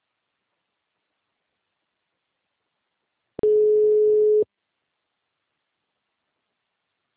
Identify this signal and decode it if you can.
Ringback tone (Europe, 425 Hz).
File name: out- -2029- - - .350